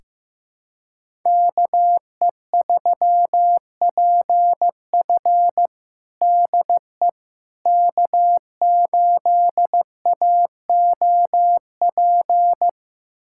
Morse code, heard as a keyed tone, about 16 wpm, 700 Hz. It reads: KE3PF DE K8AOP
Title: USA Callsign DE Callsign - 5 Characters - 15wpm